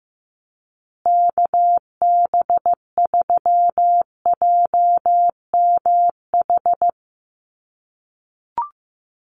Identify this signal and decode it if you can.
Morse code, heard as a keyed tone, about 15 wpm, 700 Hz. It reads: KB3JMH
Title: USA Call Signs Repeated 3x 15wpm